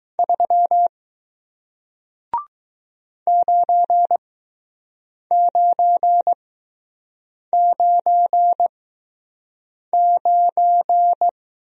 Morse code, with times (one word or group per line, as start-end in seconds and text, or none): 0.2-0.9 3
3.3-11.3 9999